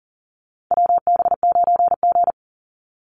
Morse code, heard as a keyed tone, about 40 wpm, 700 Hz. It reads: WB9G
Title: Call Signs - 40wpm